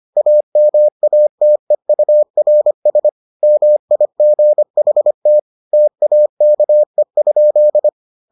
Morse code, heard as a keyed tone, about 25 wpm, 600 Hz. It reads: AMATEURS MIGHT TAKE?